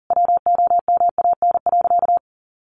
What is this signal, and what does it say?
Morse code, heard as a keyed tone, about 40 wpm, 700 Hz. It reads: WOMAN.
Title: Sentences from Top 1400 Words 40wpm